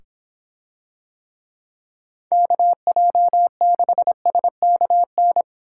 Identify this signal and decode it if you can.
Morse code, heard as a keyed tone, about 24 wpm, 700 Hz. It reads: KJ6SKN